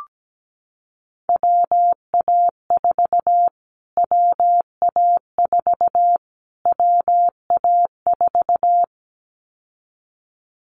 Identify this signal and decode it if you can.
Morse code, heard as a keyed tone, about 17 wpm, 700 Hz. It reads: WA4 WA4 WA4
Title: Callsign Prefixes for Top 25 DXCC - Repeated 3x - 17wpm